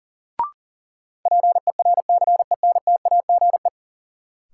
Morse code, heard as a keyed tone, about 37 wpm, 700 Hz. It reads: PERCENTAGE
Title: Sentences from Top 1300 Words - New Word Drill Alt - 40wpm